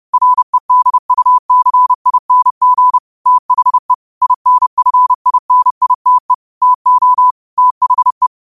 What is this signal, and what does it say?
Morse code, heard as a keyed tone, about 30 wpm, 1000 Hz. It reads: REDUCING THE INFINITE TO THE
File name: qotdfiles / 30wpm